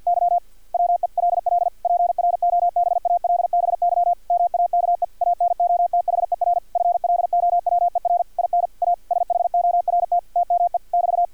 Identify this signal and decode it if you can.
Morse code, heard as a keyed tone, about 50 wpm, 700 Hz, with radio noise. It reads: Y WELL PROBALBY MAKE ANOTHER FLOWER IN A SHORT TME <BT>
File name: archived / CRSnet